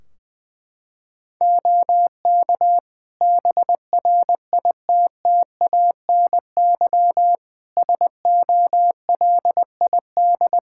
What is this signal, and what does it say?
Morse code, heard as a keyed tone, about 21 wpm, 700 Hz. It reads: OK BRITTANY SOLID
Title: K1USN SST QSO Practice 20wpm